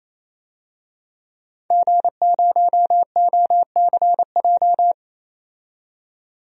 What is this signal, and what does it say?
Morse code, heard as a keyed tone, about 28 wpm, 700 Hz. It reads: G0OCJ
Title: DXCC-100 Call Signs - 5 Characters - 28wpm